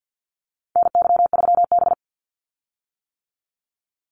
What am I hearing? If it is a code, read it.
Morse code, heard as a keyed tone, about 50 wpm, 700 Hz. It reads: NY3B
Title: Call Signs - 50wpm